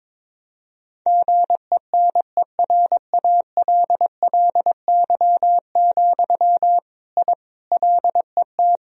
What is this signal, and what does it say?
Morse code, heard as a keyed tone, about 22 wpm, 700 Hz. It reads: GENERALLY, I LET
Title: Sentences from Top 1000 Words - Review Part 4 - 22wpm